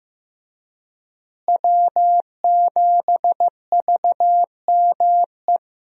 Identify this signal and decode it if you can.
Morse code, heard as a keyed tone, about 15 wpm, 700 Hz. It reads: W7VME
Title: USA Call Signs - 5 Characters - Repeated 3x - 15wpm